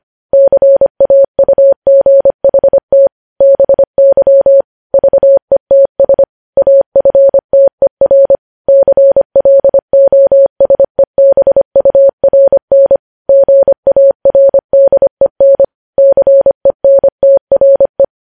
Morse code, heard as a keyed tone, about 25 wpm, 550 Hz. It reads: CAUGHT BY VETS AFTER CLOSEBURN GARDEN CENTRE
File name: podcast / News headlines-1x-25x25